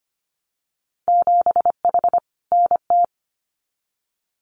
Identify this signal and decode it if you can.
Morse code, heard as a keyed tone, about 25 wpm, 700 Hz. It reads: 7H NT
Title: Winter Field Day Exchanges Encore 25wpm